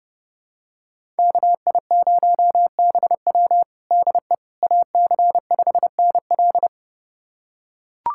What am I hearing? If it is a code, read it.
Morse code, heard as a keyed tone, about 32 wpm, 700 Hz. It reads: KI0BW DE AC5NL
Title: USA Callsign DE Callsign - 5 Characters - 30wpm